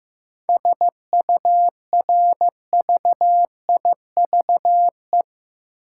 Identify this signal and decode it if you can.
Morse code, heard as a keyed tone, about 15 wpm, 700 Hz. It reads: SURVIVE
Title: Sentences from Top 1300 Words - New Word Drill - 15wpm